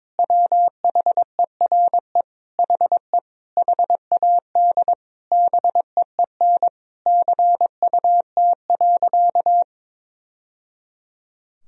Morse code, heard as a keyed tone, about 21 wpm, 700 Hz. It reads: WHERE HE HAD BEEN CUT.
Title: Sentences from Top 1000 Words - Review Part 1 - 22wpm